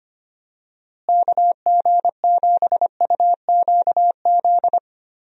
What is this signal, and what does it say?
Morse code, heard as a keyed tone, about 25 wpm, 700 Hz. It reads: KG7UQZ
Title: USA Call Signs - 6 Characters - Repeated 2x - 25wpm